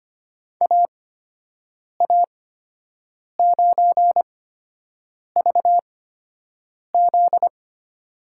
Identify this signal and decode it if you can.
Morse code, heard as a keyed tone, about 25 wpm, spaced as for 7 wpm, 700 Hz. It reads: AA9VZ
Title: USA Call Signs - 8x Character Spacing - 25wpm